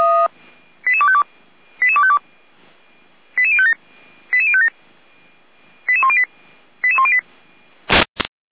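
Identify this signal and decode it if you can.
Five-tone selective call (ZVEI), heard as quick runs of five tones, tones of about 70 ms. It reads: 88252 88252 88057 88057 88108 88108
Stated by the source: ZVEI five-tone selective call received on SDR, from audio.f85360.feuerwehr-orlamuende.org (Feuerwehr Orlamünde fire brigade).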